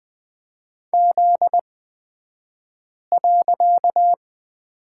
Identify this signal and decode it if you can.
Morse code, heard as a keyed tone, about 20 wpm, spaced as for 5 wpm, 700 Hz. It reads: Z.
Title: 254 Lesson 41d - Period - Two Characters 20wpm